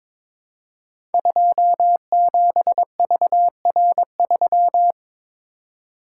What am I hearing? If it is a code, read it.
Morse code, heard as a keyed tone, about 22 wpm, 700 Hz. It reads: 27VR3
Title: Code Groups - 5 Letter:Numbers - 22wpm